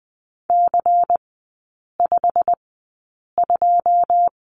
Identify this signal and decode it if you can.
Morse code, heard as a keyed tone, about 20 wpm, spaced as for 8 wpm, 700 Hz. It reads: C52
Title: 107 Lesson 18e - Letter m - Three Characters 20wpm